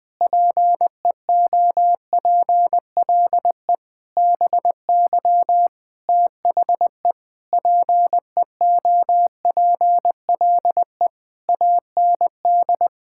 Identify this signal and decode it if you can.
Morse code, heard as a keyed tone, about 20 wpm, 700 Hz. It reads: PEOPLE BY THE PEOPLE AND
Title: Sets of 8 Words - 20wpm